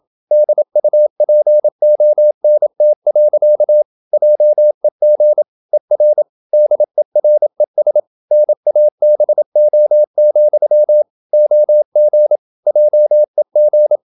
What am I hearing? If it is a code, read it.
Morse code, heard as a keyed tone, about 27 wpm, 600 Hz. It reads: DUPONT. JEG ER DERES NABO, OG JEG
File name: Ferie QRQ 27wpm